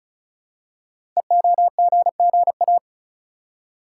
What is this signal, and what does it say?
Morse code, heard as a keyed tone, about 36 wpm, 700 Hz. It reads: EOGGA
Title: Code Groups - 5 Letters - 35wpm